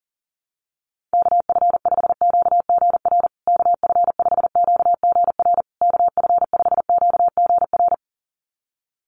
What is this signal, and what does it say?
Morse code, heard as a keyed tone, about 37 wpm, 700 Hz. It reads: KF5QGR KF5QGR KF5QGR
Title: USA Call Signs Repeated 3x 40wpm